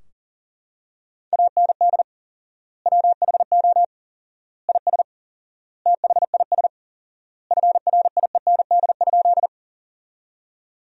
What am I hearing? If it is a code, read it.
Morse code, heard as a keyed tone, about 40 wpm, 700 Hz. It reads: AND WHO IS THIS FRIEND?
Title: 4x Word-Spacing - Sentences from Top 300 Words - 40wpm